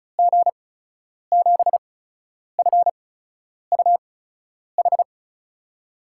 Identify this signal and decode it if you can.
Morse code, heard as a keyed tone, about 36 wpm, spaced as for 10 wpm, 700 Hz. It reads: G7FUH
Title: USA Call Signs - 6 Characters - 8x Character Spacing - 35wpm